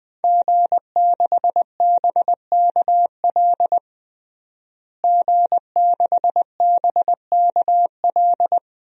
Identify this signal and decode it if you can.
Morse code, heard as a keyed tone, about 20 wpm, 700 Hz. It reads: G6BKL G6BKL
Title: DXCC-100 Call Signs - 5 Characters - Repeated 3x - 20wpm